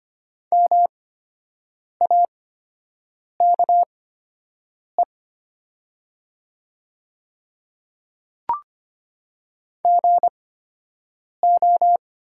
Morse code, heard as a keyed tone, about 25 wpm, spaced as for 7 wpm, 700 Hz. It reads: MAKE GO
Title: 8x Character Spacing - Top 100 Words - 25wpm